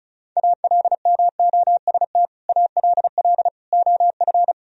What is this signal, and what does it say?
Morse code, heard as a keyed tone, about 36 wpm, 700 Hz. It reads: ALMOST ALL OF
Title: Sentences from Top 1000 Words - Review Part 4 - 35wpm